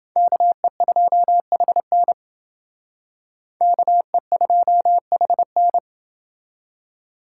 Morse code, heard as a keyed tone, about 29 wpm, 700 Hz. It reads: KE2HN KE2HN